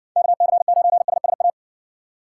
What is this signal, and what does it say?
Morse code, heard as a keyed tone, about 60 wpm, 700 Hz. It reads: KC9SSA